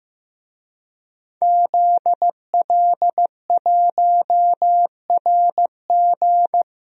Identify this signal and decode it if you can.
Morse code, heard as a keyed tone, about 15 wpm, 700 Hz. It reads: ZL1RG
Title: DXCC-100 Call Signs - 5 Characters - Repeated 2x - 15wpm